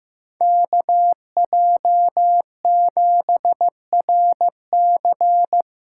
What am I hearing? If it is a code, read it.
Morse code, heard as a keyed tone, about 18 wpm, 700 Hz. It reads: KJ7RC
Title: Speed Racing - USA Call Signs 15wpm